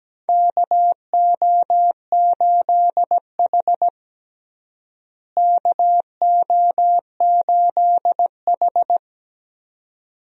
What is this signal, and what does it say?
Morse code, heard as a keyed tone, about 17 wpm, 700 Hz. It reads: KO8H KO8H
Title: USA Call Signs - 4 Characters - Repeated 2x - 17wpm